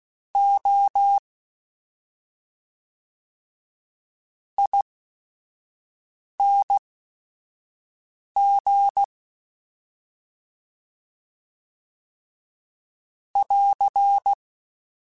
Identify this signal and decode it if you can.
Morse code, heard as a keyed tone, about 16 wpm, spaced as for 5 wpm, 800 Hz. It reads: O ING <AR>
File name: Special INGs / Special INGs-mp3s / SlowToFaster ing @5